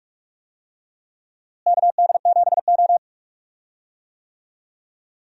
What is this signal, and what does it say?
Morse code, heard as a keyed tone, about 44 wpm, 700 Hz. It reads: KD7O
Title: USA Call Signs - 4 Characters - Repeated 3x - 45wpm